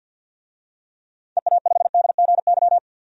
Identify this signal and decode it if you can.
Morse code, heard as a keyed tone, about 52 wpm, 700 Hz. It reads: EA5DGY
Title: DXCC-100 Call Signs - 6 Characters - 50wpm